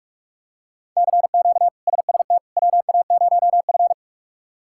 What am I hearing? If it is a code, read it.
Morse code, heard as a keyed tone, about 46 wpm, 700 Hz. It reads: CQ SST WA0F
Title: K1USN SST QSO Practice - 45wpm